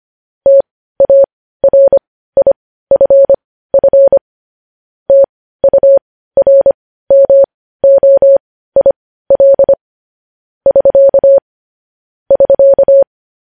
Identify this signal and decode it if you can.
Morse code, heard as a keyed tone, about 25 wpm, spaced as for 15 wpm, 550 Hz. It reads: TARIFF TURMOIL <SK> <SK>
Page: news-headlines-in-morse-code-at-15-wpm